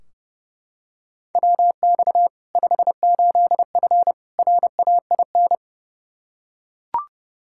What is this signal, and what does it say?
Morse code, heard as a keyed tone, about 30 wpm, 700 Hz. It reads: WX 58F RAIN
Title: Practice QSO - Exchange 2 - Info - 30wpm